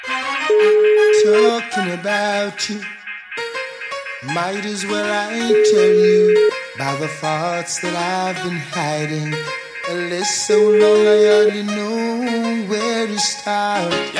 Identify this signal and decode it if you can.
Ringback tone (Europe, 425 Hz), with speech.